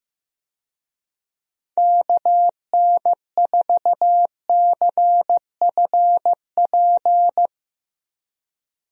Morse code, heard as a keyed tone, about 15 wpm, 700 Hz. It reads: KN4CFP
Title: USA Call Signs - 6 Characters - Repeated 3x - 15wpm